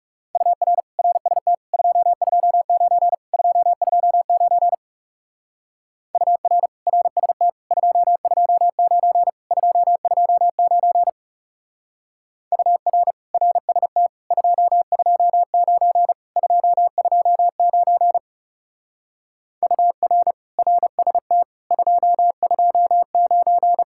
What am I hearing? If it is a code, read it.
Morse code, heard as a keyed tone, about 40 wpm, 700 Hz. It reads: UR RST 229 229 UR RST 229 229 UR RST 229 229 UR RST 229